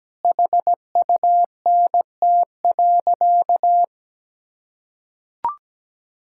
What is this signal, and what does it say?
Morse code, heard as a keyed tone, about 17 wpm, 700 Hz. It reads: HUNT.
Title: Sentences from Top 1600 Words 17wpm